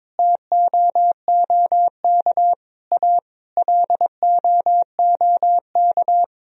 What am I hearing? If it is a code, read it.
Morse code, heard as a keyed tone, about 22 wpm, 700 Hz. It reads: TOOK A LOOK
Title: Sentences from Top 1000 Words - Review Part 1 - 22wpm